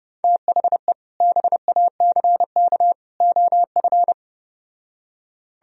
Morse code, heard as a keyed tone, about 33 wpm, 700 Hz. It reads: THE BACK OF